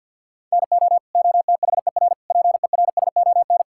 Morse code, heard as a keyed tone, about 51 wpm, 690 Hz. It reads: NO OTHER PERSON
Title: Sentences from Top 1000 Words - Review Part 5 - 50wpm